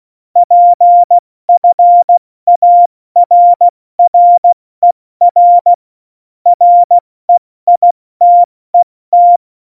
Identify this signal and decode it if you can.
Morse code, heard as a keyed tone, about 16 wpm, spaced as for 14 wpm, 700 Hz. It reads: PFARRER REITET
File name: Morsekurs / Witze / Witz 26 Mai, 2020 70 F80